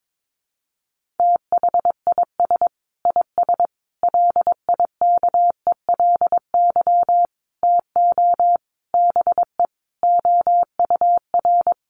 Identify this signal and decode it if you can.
Morse code, heard as a keyed tone, about 22 wpm, 700 Hz. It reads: THIS IS LIKELY TO BE OUR
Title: Sentences from Top 1400 Words 22wpm